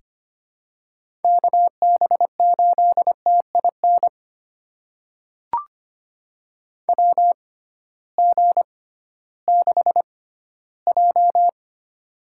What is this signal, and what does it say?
Morse code, heard as a keyed tone, about 25 wpm, spaced as for 15 wpm, 700 Hz. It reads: KB8TIN WG6J
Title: USA Call Signs - 6x Character Spacing - 25wpm